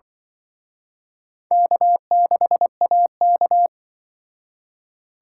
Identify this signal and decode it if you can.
Morse code, heard as a keyed tone, about 23 wpm, 700 Hz. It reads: K6AK
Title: Speed Racing - USA 1x2 Call Signs 20wpm